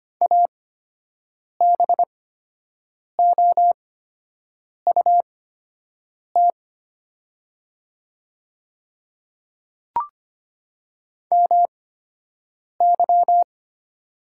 Morse code, heard as a keyed tone, about 25 wpm, spaced as for 7 wpm, 700 Hz. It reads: ABOUT MY
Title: 8x Character Spacing - Top 100 Words - 25wpm